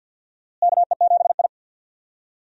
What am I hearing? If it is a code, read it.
Morse code, heard as a keyed tone, about 52 wpm, 700 Hz. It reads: KE7I
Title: USA Call Signs - 4 Characters - Repeated 2x - 50wpm